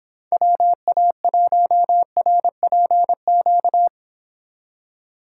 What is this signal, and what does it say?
Morse code, heard as a keyed tone, about 31 wpm, 700 Hz. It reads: WA1RPQ